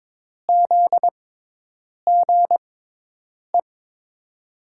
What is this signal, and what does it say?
Morse code, heard as a keyed tone, about 22 wpm, spaced as for 8 wpm, 700 Hz. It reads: ZGE
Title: USA Call Signs - 5 Characters - 6x Character Spacing - 22wpm